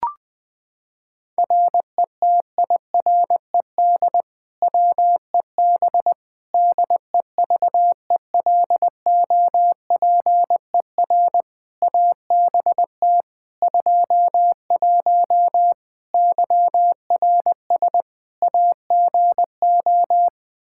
Morse code, heard as a keyed tone, about 20 wpm, 700 Hz. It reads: RETIRED WEB DEVELOPER ABT 21 YRS AGO
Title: Practice QSO - Exchange 3 - More Info - 20wpm